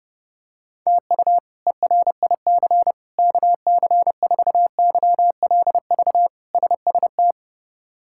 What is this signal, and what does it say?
Morse code, heard as a keyed tone, about 32 wpm, 700 Hz. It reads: TU ERIC KC4YLV SST
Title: K1USN SST QSO Practice - 30wpm